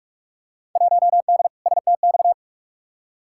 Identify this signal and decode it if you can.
Morse code, heard as a keyed tone, about 46 wpm, 700 Hz. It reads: 1D STX
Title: Field Day Exchanges - 45wpm